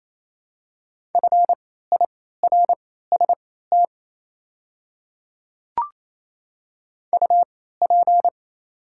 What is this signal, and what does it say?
Morse code, heard as a keyed tone, about 28 wpm, spaced as for 16 wpm, 700 Hz. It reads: FIRST UP
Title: 3x Character Spacing - Top 100 Words - 28wpm